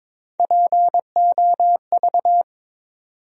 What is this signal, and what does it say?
Morse code, heard as a keyed tone, about 22 wpm, 700 Hz. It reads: POV
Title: USA Call Sign Postfixes - 3 Letters 22wpm